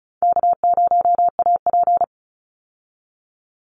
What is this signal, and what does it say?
Morse code, heard as a keyed tone, about 34 wpm, 700 Hz. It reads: K0AP
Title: Call Signs - 35wpm